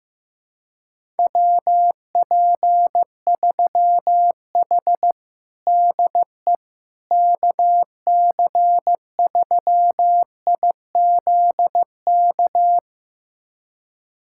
Morse code, heard as a keyed tone, about 15 wpm, 700 Hz. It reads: WP3H DE KC3IZK
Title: USA Callsign DE Callsign 15wpm